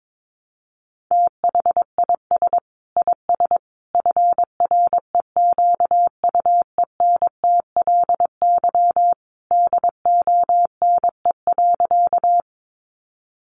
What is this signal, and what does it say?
Morse code, heard as a keyed tone, about 22 wpm, 700 Hz. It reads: THIS IS FREQUENTLY DONE.
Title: Sentences from Top 1300 Words 22wpm